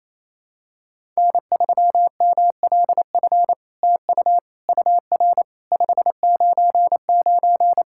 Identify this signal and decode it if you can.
Morse code, heard as a keyed tone, about 27 wpm, 700 Hz. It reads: N3MLF TU UR 599
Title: SOTA QSO Practice 28wpm